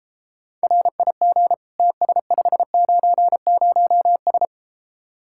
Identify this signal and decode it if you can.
Morse code, heard as a keyed tone, about 33 wpm, 700 Hz. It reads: RIG TS590S